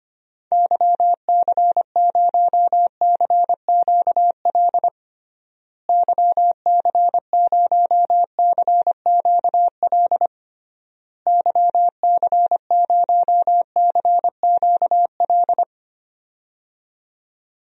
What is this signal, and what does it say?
Morse code, heard as a keyed tone, about 25 wpm, 700 Hz. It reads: YC0CQL YC0CQL YC0CQL
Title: DXCC-100 Call Signs - 6 Characters - Repeated 3x - 25wpm